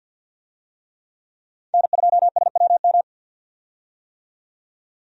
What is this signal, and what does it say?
Morse code, heard as a keyed tone, about 50 wpm, 700 Hz. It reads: N2SWM
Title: USA Call Signs - 5 Characters - 50wpm